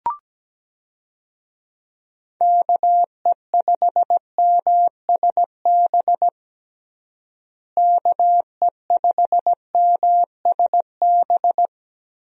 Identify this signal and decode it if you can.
Morse code, heard as a keyed tone, about 17 wpm, 700 Hz. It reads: KE5MSB KE5MSB
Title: USA Call Signs - 6 Characters - Repeated 3x - 17wpm